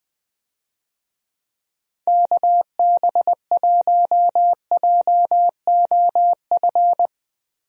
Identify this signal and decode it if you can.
Morse code, heard as a keyed tone, about 20 wpm, 700 Hz. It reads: KB1JOF
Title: USA Call Signs - 6 Characters - Repeated 2x - 20wpm